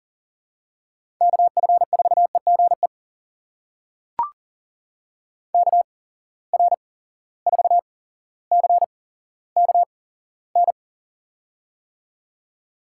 Morse code, heard as a keyed tone, about 40 wpm, spaced as for 17 wpm, 700 Hz. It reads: KF4EGE KR4CKN
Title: USA Call Signs - 8x Character Spacing - 40wpm